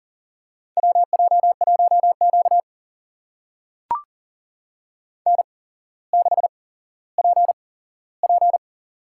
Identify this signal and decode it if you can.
Morse code, heard as a keyed tone, about 40 wpm, spaced as for 17 wpm, 700 Hz. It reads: WJ1Q N6PP
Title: USA Call Signs - 4 Characters - 8x Character Spacing - 40wpm